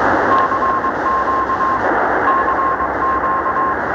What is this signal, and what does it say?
Morse code, heard as a keyed tone, about 21 wpm, 1060 Hz, with noise. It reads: CQ CQ
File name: attachments / recording